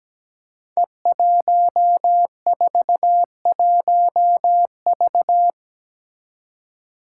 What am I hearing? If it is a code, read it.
Morse code, heard as a keyed tone, about 17 wpm, 700 Hz. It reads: E141V
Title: Five Challenging Letter-Number Combinations 17wpm